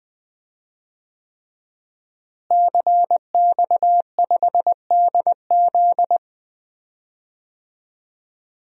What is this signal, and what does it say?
Morse code, heard as a keyed tone, about 20 wpm, 700 Hz. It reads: CX5DZ
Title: DXCC-100 Call Signs - 5 Characters - Repeated 3x - 20wpm